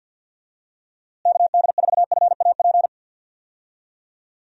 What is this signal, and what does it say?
Morse code, heard as a keyed tone, about 53 wpm, 700 Hz. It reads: KD4FAP
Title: USA Call Signs - 6 Characters - Repeated 2x - 50wpm